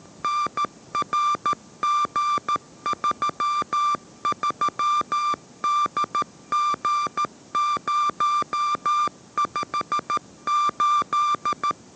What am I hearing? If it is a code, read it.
Morse code, heard as a keyed tone, about 13 wpm, 1180 Hz. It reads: NRG33DG058
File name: 14wpm